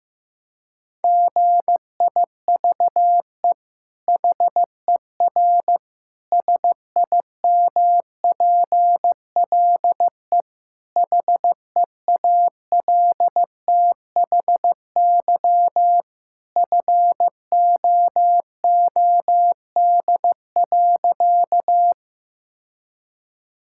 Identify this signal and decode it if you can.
Morse code, heard as a keyed tone, about 15 wpm, 700 Hz. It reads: GIVE HER SIMPLE HEALTHY FOOD.
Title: Sentences from Top 1000 Words - Review Part 5 - 15wpm